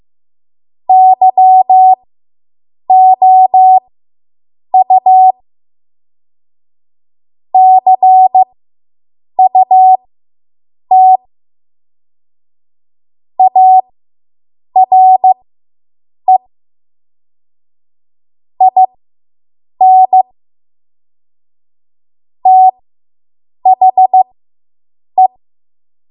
Morse code, heard as a keyed tone, about 15 wpm, spaced as for 7 wpm, 750 Hz. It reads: YOU CUT ARE IN THE